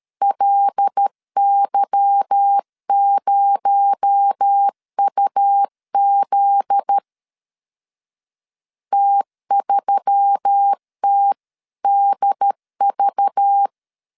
Morse code, heard as a keyed tone, about 13 wpm, 800 Hz. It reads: LY0UZ T3TDV
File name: LEZIONI / 8 YJ45 / 8 Y 12wpm 2228